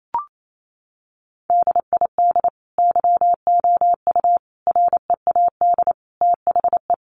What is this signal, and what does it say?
Morse code, heard as a keyed tone, about 28 wpm, 700 Hz. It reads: DID YOU READ THE
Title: Sentences from Top 1200 Words 28wpm